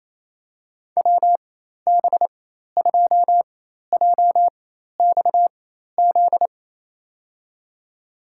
Morse code, heard as a keyed tone, about 28 wpm, spaced as for 13 wpm, 700 Hz. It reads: WB2JXZ
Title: USA Call Signs - 6 Characters - 4x Character Spacing - 28wpm